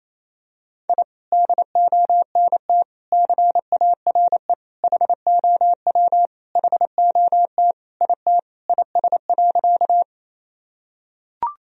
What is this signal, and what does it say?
Morse code, heard as a keyed tone, about 28 wpm, 700 Hz. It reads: I DONT CARE HOW HOT IT IS.
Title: Sentences from Top 1000 Words - Review Part 3 - 28wpm